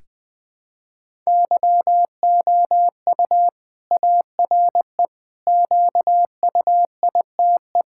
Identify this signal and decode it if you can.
Morse code, heard as a keyed tone, about 20 wpm, 700 Hz. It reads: YOU ARE QUITE
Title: Sentences from Top 1000 Words - Review Part 5 - 20wpm